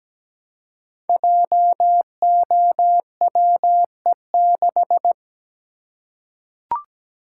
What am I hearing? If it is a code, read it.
Morse code, heard as a keyed tone, about 17 wpm, 700 Hz. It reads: JOWE6
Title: Code Groups - 5 Letter:Numbers - 17wpm